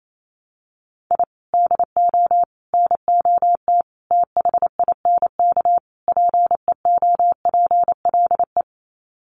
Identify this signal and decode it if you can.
Morse code, heard as a keyed tone, about 28 wpm, 700 Hz. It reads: I DO NOT THINK PEOPLE
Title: Sets of 5 Words Encore2 28wpm